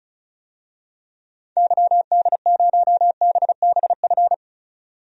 Morse code, heard as a keyed tone, about 34 wpm, 700 Hz. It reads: YD0BBF
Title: DXCC-100 Call Signs - 6 Characters - 35wpm